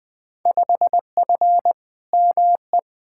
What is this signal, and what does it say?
Morse code, heard as a keyed tone, about 20 wpm, 700 Hz. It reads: 5F ME
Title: Field Day Exchanges Encore - 20wpm